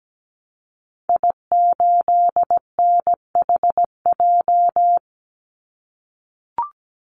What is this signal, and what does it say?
Morse code, heard as a keyed tone, about 17 wpm, 700 Hz. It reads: I8NHJ
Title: Call Signs - 17wpm